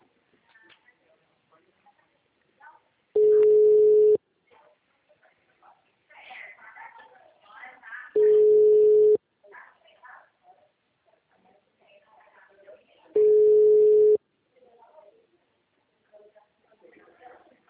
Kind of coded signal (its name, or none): telephone call progress tone